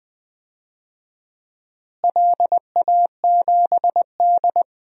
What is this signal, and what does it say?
Morse code, heard as a keyed tone, about 20 wpm, 700 Hz. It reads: LA7D
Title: DXCC-100 Call Signs - 4 Characters - Repeated 2x - 20wpm